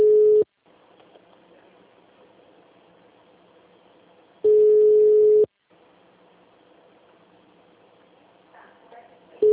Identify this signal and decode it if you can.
Ringback tone (Europe, 425 Hz), with speech.
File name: out- -2016- - - .5268